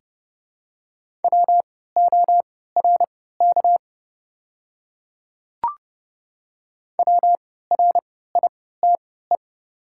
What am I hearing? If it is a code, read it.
Morse code, heard as a keyed tone, about 30 wpm, spaced as for 17 wpm, 690 Hz. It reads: WORK WRITE
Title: 3x Character Spacing - Top 200 Words - 30wpm